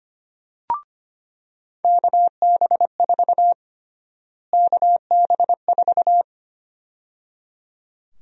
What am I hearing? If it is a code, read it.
Morse code, heard as a keyed tone, about 25 wpm, 700 Hz. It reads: KB4 KB4
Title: USA Call Sign Prefixes - Repeated 2x - 25wpm